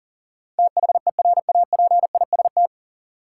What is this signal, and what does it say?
Morse code, heard as a keyed tone, about 40 wpm, 700 Hz. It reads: THERAPIST
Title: Practice QSO - Exchange 3 - More Info - 40wpm